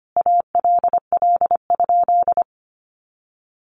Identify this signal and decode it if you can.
Morse code, heard as a keyed tone, about 25 wpm, 700 Hz. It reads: ALL?
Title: Sentences from Top 1500 Words 25wpm